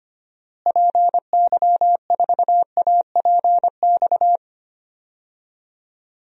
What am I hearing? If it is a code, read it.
Morse code, heard as a keyed tone, about 25 wpm, 700 Hz. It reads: PY4APX
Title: DXCC-100 Call Signs - 6 Characters - Repeated 3x - 25wpm